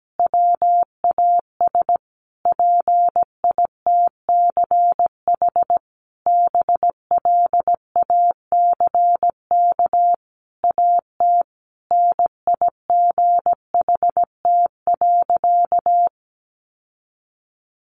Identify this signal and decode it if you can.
Morse code, heard as a keyed tone, about 17 wpm, 700 Hz. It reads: WAS PITCH BLACK AT NIGHT.
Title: Sentences from Top 1300 Words 17wpm